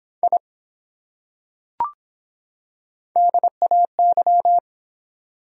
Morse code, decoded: I DAY

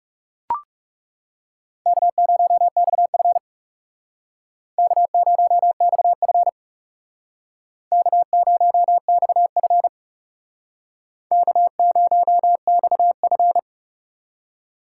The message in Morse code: K0XF K0XF K0XF K0XF

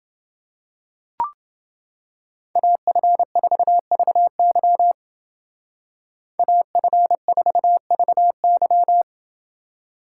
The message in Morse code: AF4VY AF4VY

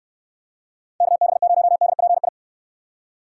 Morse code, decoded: KB0DPI

